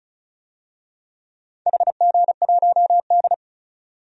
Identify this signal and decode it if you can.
Morse code, heard as a keyed tone, about 36 wpm, 700 Hz. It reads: HG1D
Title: DXCC-100 Call Signs - 4 Characters - 35wpm